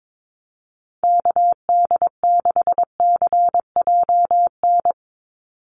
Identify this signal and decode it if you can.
Morse code, heard as a keyed tone, about 22 wpm, 700 Hz. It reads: KD6CJN
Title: USA Call Signs 22wpm